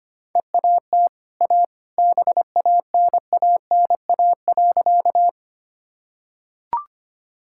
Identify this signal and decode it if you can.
Morse code, heard as a keyed tone, about 25 wpm, 700 Hz. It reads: EAT A BANANA.
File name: Easy Sentences with Hints - 25wpm